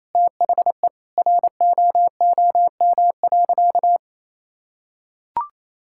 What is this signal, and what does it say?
Morse code, heard as a keyed tone, about 28 wpm, 700 Hz. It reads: THE ROOM.
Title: Sentences from Top 1000 Words - Review Part 6 - 28wpm